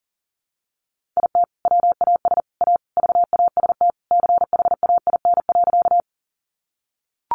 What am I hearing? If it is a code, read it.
Morse code, heard as a keyed tone, about 40 wpm, 700 Hz. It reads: IT WAS A VAST CHAIN.
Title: Sentences from Top 1500 Words 40wpm